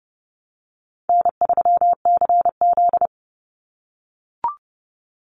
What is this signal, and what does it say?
Morse code, heard as a keyed tone, about 30 wpm, 700 Hz. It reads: N3CZ
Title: Easy Call Signs - 30wpm